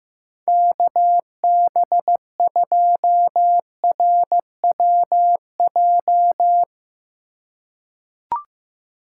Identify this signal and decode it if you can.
Morse code, heard as a keyed tone, about 15 wpm, 700 Hz. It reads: KB2RWJ
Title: USA Call Signs - 6x Character Spacing - 15wpm